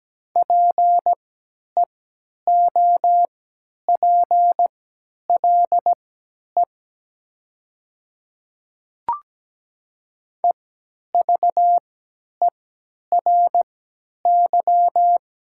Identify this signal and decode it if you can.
Morse code, heard as a keyed tone, about 17 wpm, spaced as for 10 wpm, 700 Hz. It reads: PEOPLE EVERY
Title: 3x Character Spacing - Top 300 Words - 17wpm